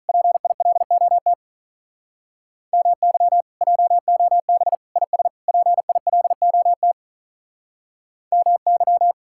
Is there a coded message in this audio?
Yes: Morse code from 0.1 to 9.1 s, heard as a keyed tone, about 46 wpm, 700 Hz. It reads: PILOT MY JOB IS PILOT MY